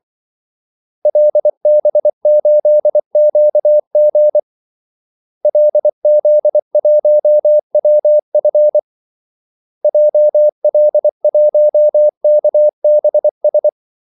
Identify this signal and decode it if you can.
Morse code, heard as a keyed tone, about 24 wpm, 600 Hz. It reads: LB8QG LZ1WF JL1KBS
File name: Callsigns-24wpm